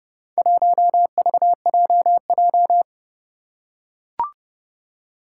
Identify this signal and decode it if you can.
Morse code, heard as a keyed tone, about 30 wpm, 700 Hz. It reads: 1VJJ
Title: Four Challenging Letter-Number Combinations 30wpm